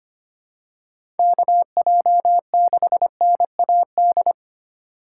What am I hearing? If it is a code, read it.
Morse code, heard as a keyed tone, about 25 wpm, 700 Hz. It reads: KJ6NAD
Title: USA Call Signs - 6 Characters - 4x Character Spacing - 25wpm